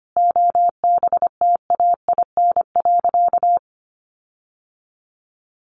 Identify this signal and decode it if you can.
Morse code, heard as a keyed tone, about 25 wpm, 700 Hz. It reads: OBTAIN.
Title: Sentences from Top 1300 Words 25wpm